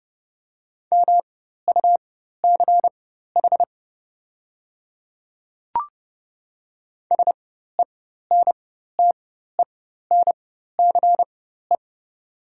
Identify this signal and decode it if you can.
Morse code, heard as a keyed tone, about 29 wpm, spaced as for 14 wpm, 700 Hz. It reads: MUCH SENTENCE
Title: 4x Character Spacing - Top 200 Words - 30wpm